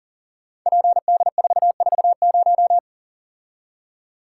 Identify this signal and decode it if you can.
Morse code, heard as a keyed tone, about 40 wpm, 700 Hz. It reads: PD440
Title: Code Groups - 5 Letter:Numbers - 40wpm